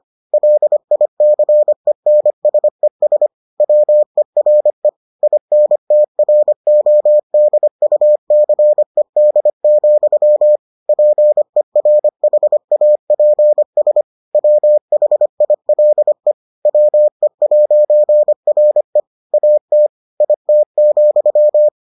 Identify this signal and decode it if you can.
Morse code, heard as a keyed tone, about 25 wpm, 600 Hz. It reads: LICENSES WERE INTRODUCED, PERHAPS WHILE WE'RE AT IT,